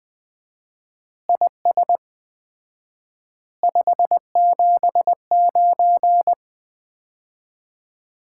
Morse code, heard as a keyed tone, about 20 wpm, 700 Hz. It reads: IS 579
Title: Common QSO Phrases - 4x Word-Spacing - 20wpm